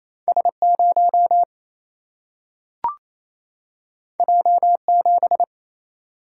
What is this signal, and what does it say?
Morse code, heard as a keyed tone, about 28 wpm, 700 Hz. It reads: S 0 J 7
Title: Single-Letter Single-Number 28wpm